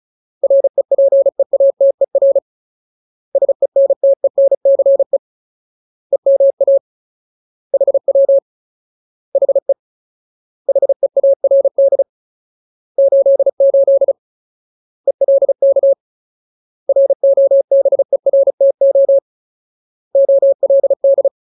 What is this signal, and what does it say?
Morse code, heard as a keyed tone, about 34 wpm, 550 Hz. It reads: REPEATER SENTENCE EMA HW HE HEARD 88 ELK ROBERTO OLD
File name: Ultimate Word List / 35WPM Ultimate